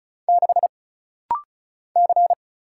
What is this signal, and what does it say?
Morse code, heard as a keyed tone, about 36 wpm, 700 Hz. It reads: 6 C
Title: Single Letter-Number - Rapid-Fire 35wpm